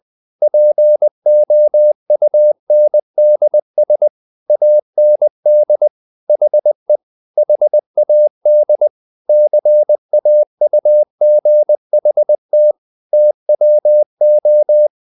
Morse code, heard as a keyed tone, about 20 wpm, 600 Hz. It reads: POUNDS AND HE HAD CAUGHT TWO
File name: the old man and the sea / the old man and the sea part 1 20wpm